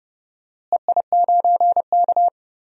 Morse code, heard as a keyed tone, about 30 wpm, 700 Hz. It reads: EI9K
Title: DXCC-100 Call Signs - 4 Characters - Repeated 3x - 30wpm